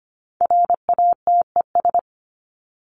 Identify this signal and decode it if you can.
Morse code, heard as a keyed tone, about 25 wpm, 700 Hz. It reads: RATES
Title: ES:S Suffix - Top 100 Words with Base Words 25wpm